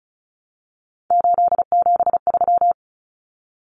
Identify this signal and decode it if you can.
Morse code, heard as a keyed tone, about 34 wpm, 700 Hz. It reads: 873
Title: Common Cut Numbers - 3-Digit 35wpm